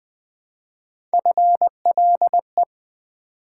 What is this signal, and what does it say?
Morse code, heard as a keyed tone, about 20 wpm, 700 Hz. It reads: FLE
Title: Build A Word - 1 Letter At A Time - 3 to 5-Letter Words - 20wpm